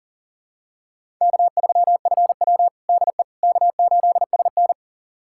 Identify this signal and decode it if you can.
Morse code, heard as a keyed tone, about 37 wpm, 700 Hz. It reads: K3FW DE K8SN
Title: USA Callsign DE Callsign - 4 Characters - 40wpm